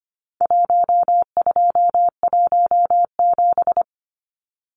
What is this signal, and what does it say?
Morse code, heard as a keyed tone, about 25 wpm, 700 Hz. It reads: 1217
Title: Numbers - Contest Style - 4-Digit 25wpm